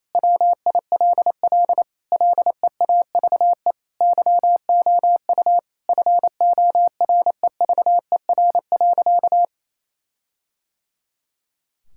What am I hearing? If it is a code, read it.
Morse code, heard as a keyed tone, about 28 wpm, 700 Hz. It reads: WILL LEAVE YOU FOREVER.
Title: Sentences from Top 1600 Words 28wpm